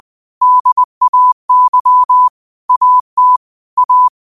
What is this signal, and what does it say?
Morse code, heard as a keyed tone, about 20 wpm, 1000 Hz. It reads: DAY AT A
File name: qotdfiles / 20wpm